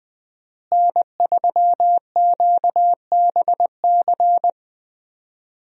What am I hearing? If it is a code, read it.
Morse code, heard as a keyed tone, about 20 wpm, 700 Hz. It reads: N3QBC
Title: USA Call Signs - 5 Characters - Repeated 2x - 20wpm